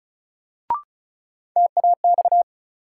Morse code, heard as a keyed tone, about 36 wpm, 700 Hz. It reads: TAX
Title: Sentences from Top 1000 Words - Review Part 5 - New Word Drill - 35wpm